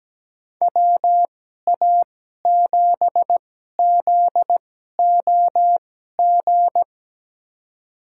Morse code, heard as a keyed tone, about 17 wpm, spaced as for 12 wpm, 700 Hz. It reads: WA7ZOG
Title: USA Call Signs - 6 Characters - 2x Character Spacing - 17wpm